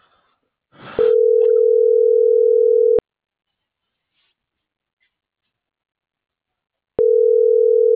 Ringback tone (North America), 440 + 480 Hz.